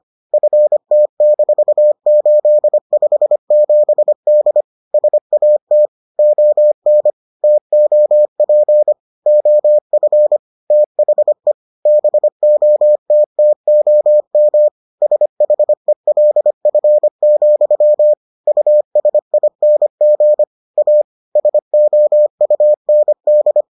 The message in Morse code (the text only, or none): FT-857D SAT ON TOP OF THE BOTTOM SHELF, USING A SOUND